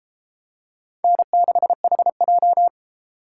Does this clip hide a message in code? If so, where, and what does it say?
Yes: Morse code from 1.0 to 2.7 s, heard as a keyed tone, about 31 wpm, 700 Hz. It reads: N6HJ